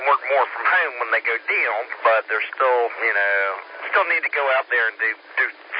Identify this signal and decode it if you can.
Emergency Alert System (SAME) tones, heard as harsh buzzing bursts, with speech. It is heard from 1.0 to 5.2 s.